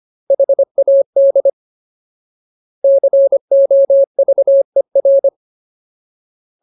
Morse code, heard as a keyed tone, about 25 wpm, 550 Hz. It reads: HAD COVER
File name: Ultimate Word List / 25WPM Ultimate